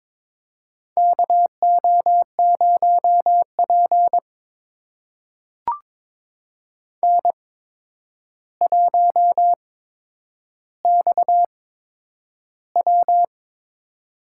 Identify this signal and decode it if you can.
Morse code, heard as a keyed tone, about 22 wpm, spaced as for 9 wpm, 700 Hz. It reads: KO0P N1XW
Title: USA Call Signs - 4 Characters - 8x Character Spacing - 22wpm